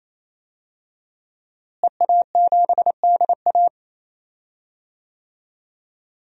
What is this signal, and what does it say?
Morse code, heard as a keyed tone, about 28 wpm, 700 Hz. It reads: EA7DA
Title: DXCC-100 Call Signs - 5 Characters - Repeated 3x - 28wpm